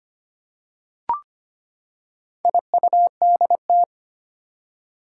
Morse code, heard as a keyed tone, about 25 wpm, 700 Hz. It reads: IUDT
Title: Four Challenging Letter-Number Combinations 25wpm